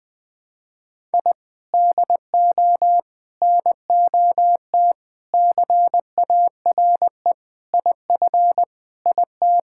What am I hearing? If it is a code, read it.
Morse code, heard as a keyed tone, about 21 wpm, 700 Hz. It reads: I DO NOT CARE IF IT
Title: Sentences with 5-Word Phrases - Spoken Beginning - 20wpm